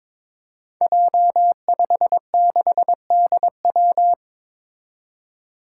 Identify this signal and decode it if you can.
Morse code, heard as a keyed tone, about 22 wpm, 700 Hz. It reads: J56DW
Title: Five Challenging Letter-Number Combinations 22wpm